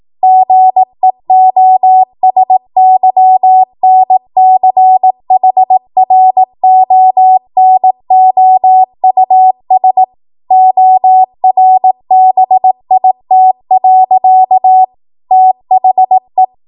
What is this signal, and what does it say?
Morse code, heard as a keyed tone, about 18 wpm, 750 Hz. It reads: GEOSYNCHRONOUS ORBIT. THE
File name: Morse / 18WPM